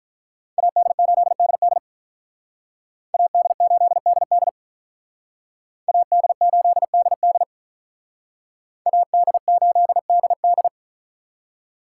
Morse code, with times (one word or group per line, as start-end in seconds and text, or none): 0.6-1.8 AD8DD
3.1-4.5 AD8DD
5.9-7.4 AD8DD
8.9-10.7 AD8DD